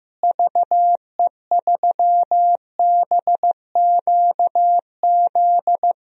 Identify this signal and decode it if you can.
Morse code, heard as a keyed tone, about 15 wpm, 700 Hz. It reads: VE3BQZ
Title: DXCC-100 Call Signs - 6 Characters - Repeated 3x - 15wpm